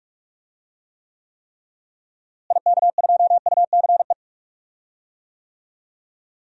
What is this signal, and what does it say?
Morse code, heard as a keyed tone, about 46 wpm, 700 Hz. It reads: IK2UCE
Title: DXCC-100 Call Signs - 6 Characters - Repeated 3x - 45wpm